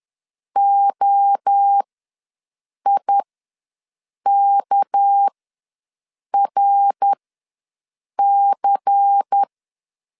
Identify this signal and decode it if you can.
Morse code, heard as a keyed tone, about 10 wpm, spaced as for 6 wpm, 800 Hz. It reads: OIKRC
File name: LEZIONI / 6 PKX0 / 6 PK ws 10wpm 3745